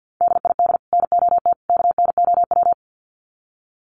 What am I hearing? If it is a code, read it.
Morse code, heard as a keyed tone, about 48 wpm, 700 Hz. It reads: DID NOT KNOW
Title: Sets of 3 Words 50wpm